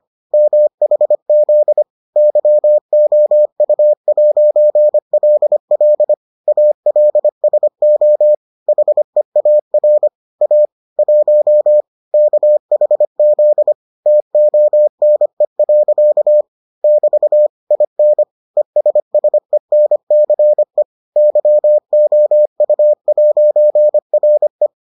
Morse code, heard as a keyed tone, about 25 wpm, 600 Hz. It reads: MHZ YOU'LL ALSO HEAR A 1 KHZ TONE. <BT> IN ESSENCE YOU'RE